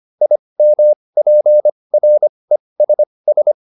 Morse code, heard as a keyed tone, about 25 wpm, spaced as for 20 wpm, 600 Hz. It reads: IMPRESS